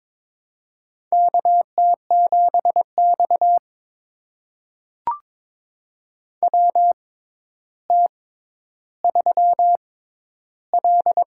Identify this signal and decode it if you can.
Morse code, heard as a keyed tone, about 22 wpm, spaced as for 11 wpm, 700 Hz. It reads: KT7X WT3L
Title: USA Call Signs - 4 Characters - 6x Character Spacing - 22wpm